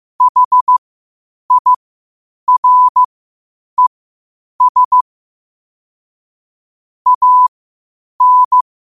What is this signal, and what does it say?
Morse code, heard as a keyed tone, about 15 wpm, spaced as for 8 wpm, 1000 Hz. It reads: HIRES AN
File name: qotdfiles / 8wpm